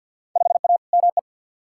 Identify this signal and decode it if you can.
Morse code, heard as a keyed tone, about 51 wpm, 700 Hz. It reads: 5A ME
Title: Field Day Exchanges Encore - 50wpm